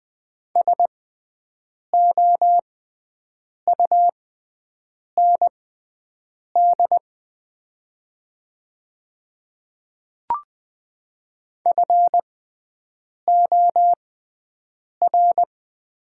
Morse code, heard as a keyed tone, about 20 wpm, spaced as for 7 wpm, 700 Hz. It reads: SOUND FOR
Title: 6x Character Spacing - Top 100 Words - 20wpm